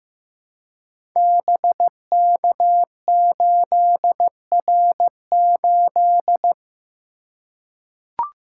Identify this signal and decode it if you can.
Morse code, heard as a keyed tone, about 15 wpm, 700 Hz. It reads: BK8R8
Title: Code Groups - 5 Letter:Numbers - 15wpm